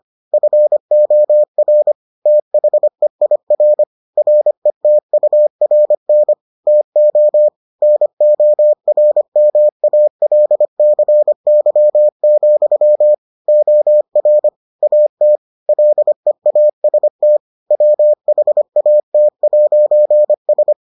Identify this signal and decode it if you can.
Morse code, heard as a keyed tone, about 25 wpm, 600 Hz. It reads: FOR THEIR RETURN TO NORMALCY, OR AT LEAST WHAT'S